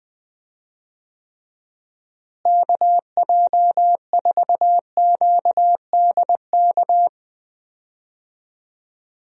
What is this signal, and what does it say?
Morse code, heard as a keyed tone, about 20 wpm, 700 Hz. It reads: KJ4QDK
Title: USA Call Signs - 6 Characters - Repeated 3x - 20wpm